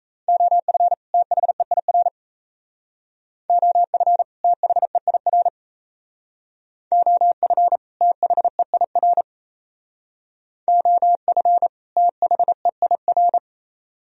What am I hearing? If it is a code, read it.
Morse code, heard as a keyed tone, about 33 wpm, 700 Hz. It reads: OF THEIR OF THEIR OF THEIR OF THEIR
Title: Speed-Racing - Sets of 2 Words 28wpm